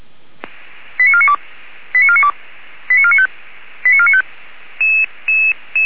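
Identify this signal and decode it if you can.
Five-tone selective call (ZVEI), heard as quick runs of five tones, tones of about 70 ms. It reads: 89492 89492 89495 89495
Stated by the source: ZVEI five-tone selective call received by SDR, fire brigade alerting.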